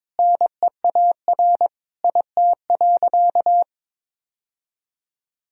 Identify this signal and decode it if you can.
Morse code, heard as a keyed tone, about 22 wpm, 700 Hz. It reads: NEAR IT.
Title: Sentences from Top 1000 Words - Review Part 1 - 22wpm